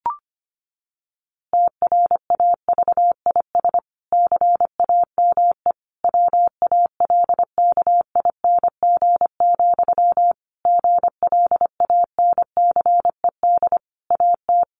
Morse code, heard as a keyed tone, about 25 wpm, 700 Hz. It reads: TRAVIS CAME WALKING, GLANCED AT
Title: A Sound Of Thunder 25wpm